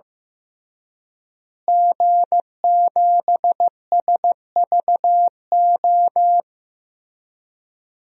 Morse code, heard as a keyed tone, about 15 wpm, 700 Hz. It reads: G7SVO
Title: DXCC-100 Call Signs - 5 Characters - Repeated 3x - 15wpm